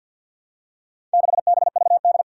Morse code, heard as a keyed tone, about 52 wpm, 700 Hz. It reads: 6BVD
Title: Four Challenging Letter-Number Combinations 50wpm